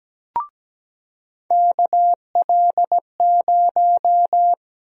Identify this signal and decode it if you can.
Morse code, heard as a keyed tone, about 17 wpm, 700 Hz. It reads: KL0
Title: USA Call Sign Prefixes - 17wpm